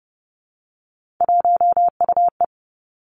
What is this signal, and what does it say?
Morse code, heard as a keyed tone, about 30 wpm, 700 Hz. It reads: 1UE
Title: Three Challenging Letter-Number Combinations 30wpm